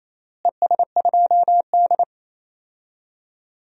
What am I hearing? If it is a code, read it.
Morse code, heard as a keyed tone, about 27 wpm, 700 Hz. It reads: ES2D
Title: DXCC-100 Call Signs - 4 Characters - Repeated 3x - 28wpm